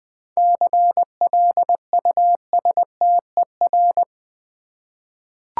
Morse code, heard as a keyed tone, about 20 wpm, 700 Hz. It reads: CLUSTER
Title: 7 Letter Words - First 3 Letters Given - 20wpm